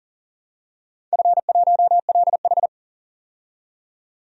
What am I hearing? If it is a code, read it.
Morse code, heard as a keyed tone, about 40 wpm, 700 Hz. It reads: F1LH
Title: DXCC-100 Call Signs - 4 Characters - 40wpm